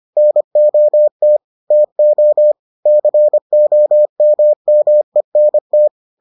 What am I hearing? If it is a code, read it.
Morse code, heard as a keyed tone, about 25 wpm, 600 Hz. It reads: NOT TO COMMENT